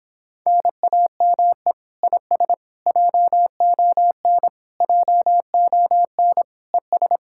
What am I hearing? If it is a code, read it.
Morse code, heard as a keyed tone, about 26 wpm, 700 Hz. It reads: NAME IS JON JON ES